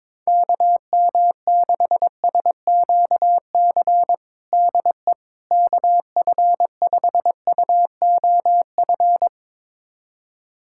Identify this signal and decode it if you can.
Morse code, heard as a keyed tone, about 21 wpm, 700 Hz. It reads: KM6SQC DE KF5UOF
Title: USA Callsign DE Callsign - 6 Characters - 22wpm